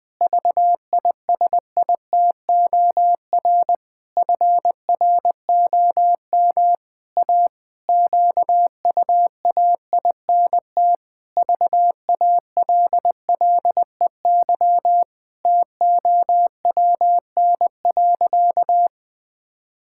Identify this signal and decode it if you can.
Morse code, heard as a keyed tone, about 21 wpm, 700 Hz. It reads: VISITOR FROM A QUAINT VALLEY TOWN.
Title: Pangram Challenge v2 - 20wpm